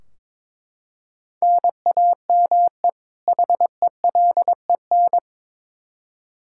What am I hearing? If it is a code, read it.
Morse code, heard as a keyed tone, about 22 wpm, 700 Hz. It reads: NAME HELEN
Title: Practice QSO - Exchange 1 - Beginning - 22wpm